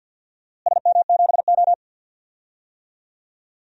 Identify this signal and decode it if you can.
Morse code, heard as a keyed tone, about 48 wpm, 700 Hz. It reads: SM7O
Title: DXCC-100 Call Signs - 4 Characters - Repeated 3x - 50wpm